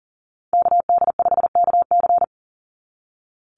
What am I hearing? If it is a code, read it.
Morse code, heard as a keyed tone, about 40 wpm, 700 Hz. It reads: KD5KC
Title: NA SOTA Activator Callsigns 40wpm